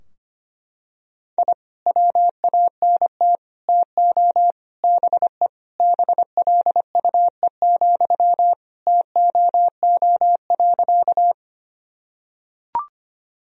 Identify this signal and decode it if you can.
Morse code, heard as a keyed tone, about 25 wpm, 700 Hz. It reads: I WANT TO BE BLUE, TOO.
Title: Sentences from Top 1000 Words - Review Part 3 - 25wpm